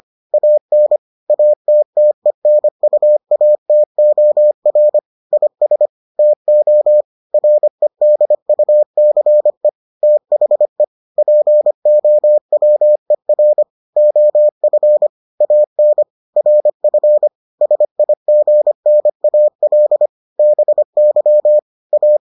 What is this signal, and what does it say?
Morse code, heard as a keyed tone, about 25 wpm, 600 Hz. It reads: AN ATTENUATOR IS TO REDUCE THE POWER OF AN RF SIGNAL BY A